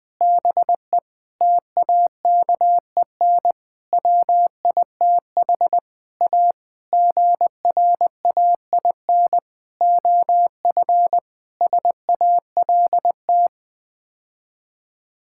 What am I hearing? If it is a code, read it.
Morse code, heard as a keyed tone, about 20 wpm, 700 Hz. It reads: BE TAKEN WITH A GRAIN OF SALT
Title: Sets of 8 Words - 20wpm